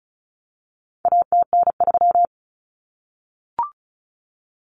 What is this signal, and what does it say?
Morse code, heard as a keyed tone, about 34 wpm, 700 Hz. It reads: ATN3
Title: Common Cut Numbers - 4-Digit 35wpm